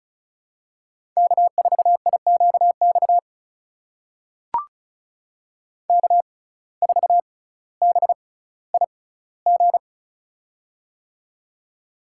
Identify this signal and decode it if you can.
Morse code, heard as a keyed tone, about 36 wpm, spaced as for 18 wpm, 700 Hz. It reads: K4IQX K4BIG
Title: USA Call Signs - 5 Characters - 6x Character Spacing - 35wpm